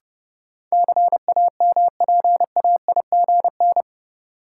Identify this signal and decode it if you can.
Morse code, heard as a keyed tone, about 32 wpm, 700 Hz. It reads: CAMPAIGN
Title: Sentences from Top 1200 Words - New Word Drill - 30wpm